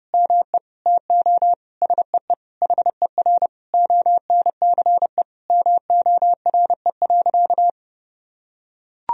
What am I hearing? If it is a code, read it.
Morse code, heard as a keyed tone, about 30 wpm, 700 Hz. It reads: ME TO SEE HER ONCE MORE.
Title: Sentences from Top 500 Words and Abbreviations 30wpm